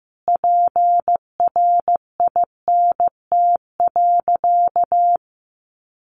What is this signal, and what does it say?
Morse code, heard as a keyed tone, about 14 wpm, 700 Hz. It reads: PRINT.
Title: Sentences from Top 1400 Words 15wpm